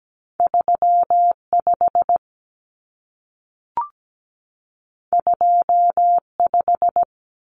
Morse code, heard as a keyed tone, about 17 wpm, 700 Hz. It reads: 35 25
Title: Numbers - 2-Digit 17wpm